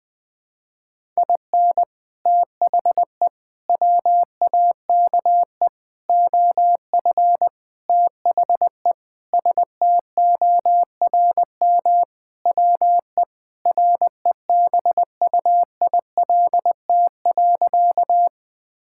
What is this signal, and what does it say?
Morse code, heard as a keyed tone, about 21 wpm, 700 Hz. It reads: IN THE WAKE OF THE STORM WE REBUILT.
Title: Sentences with 5-Word Phrases - 20wpm